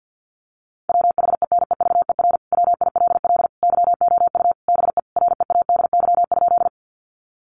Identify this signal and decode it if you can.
Morse code, heard as a keyed tone, about 50 wpm, 700 Hz. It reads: WHENEVER WILL YOU BE READY?
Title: Sentences from Top 1200 Words 50wpm